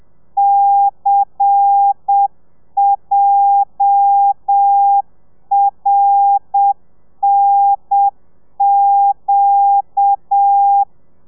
Morse code, heard as a keyed tone, about 7 wpm, 800 Hz. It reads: CJRNQ
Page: Lezione 1: audio a 30 CPM